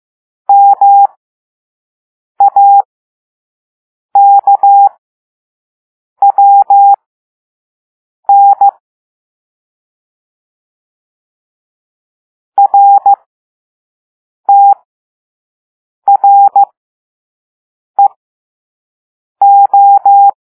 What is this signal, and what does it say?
Morse code, heard as a keyed tone, about 15 wpm, spaced as for 5 wpm, 800 Hz. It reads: MAKWN RTREO
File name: Morse / Level5-All-B